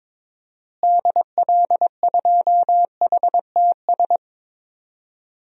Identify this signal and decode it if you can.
Morse code, heard as a keyed tone, about 21 wpm, 700 Hz. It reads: DL2HTS
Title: DXCC-100 Call Signs - 6 Characters - Repeated 3x - 22wpm